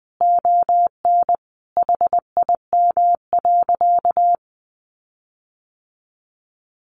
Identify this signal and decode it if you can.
Morse code, heard as a keyed tone, about 20 wpm, 700 Hz. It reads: ON HIM.
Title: Sentences from Top 1500 Words 20wpm